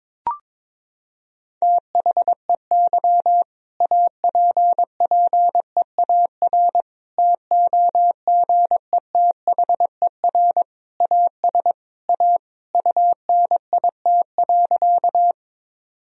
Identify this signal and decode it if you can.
Morse code, heard as a keyed tone, about 22 wpm, 700 Hz. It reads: THEY APPEAR TOGETHER AS A UNIT.
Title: Sentences from Top 1000 Words - Review Part 2 - 22wpm